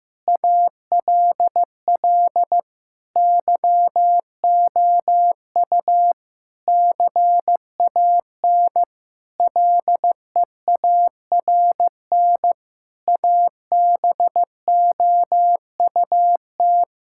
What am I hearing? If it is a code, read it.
Morse code, heard as a keyed tone, about 16 wpm, 700 Hz. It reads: ALL YOU CAN LEARN ABOUT
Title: Sentences from Top 1000 Words - Review Part 5 - 15wpm